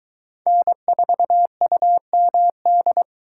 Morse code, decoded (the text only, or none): N4UMD